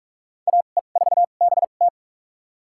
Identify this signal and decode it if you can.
Morse code, heard as a keyed tone, about 43 wpm, spaced as for 32 wpm, 700 Hz. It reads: AE4BT
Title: USA Call Signs - 5 Characters - 2x Character Spacing - 45wpm